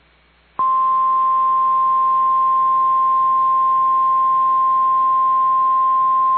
Emergency Alert System (SAME) tones (the 1050 Hz attention tone). It is heard from 0.6 to 5.8 s.